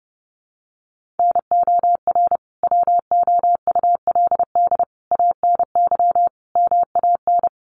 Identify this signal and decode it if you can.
Morse code, heard as a keyed tone, about 30 wpm, 700 Hz. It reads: NOR WOULD ANY MAN
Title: Sentences from Top 1500 Words 30wpm